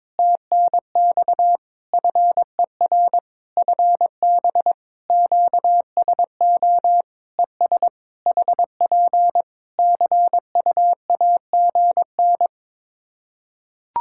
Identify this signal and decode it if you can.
Morse code, heard as a keyed tone, about 22 wpm, 700 Hz. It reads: TNX FER FB QSO ES HP CUAGN
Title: Practice QSO - Exchange 4 - Ending - 22wpm